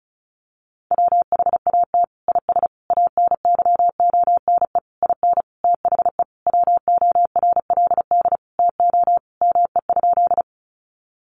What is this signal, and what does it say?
Morse code, heard as a keyed tone, about 34 wpm, 700 Hz. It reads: WHAT IS ANYONE IN THE WORLD TO ME?
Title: Sentences from Top 900 Words 35wpm